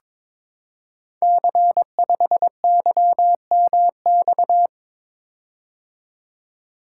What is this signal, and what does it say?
Morse code, heard as a keyed tone, about 22 wpm, 700 Hz. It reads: C5YMX
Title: Code Groups - 5 Letter:Numbers - 22wpm